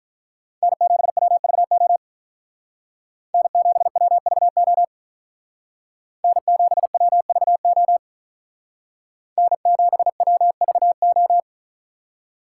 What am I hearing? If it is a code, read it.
Morse code, heard as a keyed tone, about 43 wpm, 700 Hz. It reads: N7WVO N7WVO N7WVO N7WVO
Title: Speed Racing - USA 1x3 Call Signs 35wpm